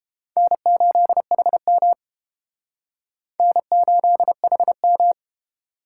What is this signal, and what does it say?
Morse code, heard as a keyed tone, about 31 wpm, 700 Hz. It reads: N8HM N8HM